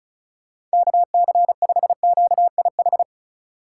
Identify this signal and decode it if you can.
Morse code, heard as a keyed tone, about 36 wpm, 700 Hz. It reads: KC5QIH
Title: USA Call Signs - 4x Character Spacing - 35wpm